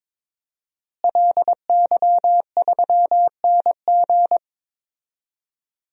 Morse code, heard as a keyed tone, about 22 wpm, 700 Hz. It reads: LY3NG
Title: DXCC-100 Call Signs - 5 Characters - Repeated 2x - 22wpm